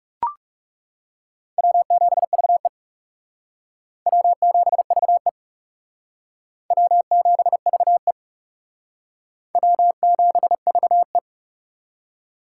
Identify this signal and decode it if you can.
Morse code, heard as a keyed tone, about 36 wpm, 700 Hz. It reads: W7VE W7VE W7VE W7VE
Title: Speed Racing - USA 1x2 Call Signs 30wpm